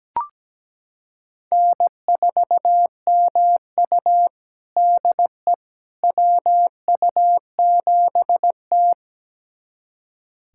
Morse code, heard as a keyed tone, about 17 wpm, 700 Hz. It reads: N4MU DE WU7T
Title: USA Callsign DE Callsign - 4 Characters - 17wpm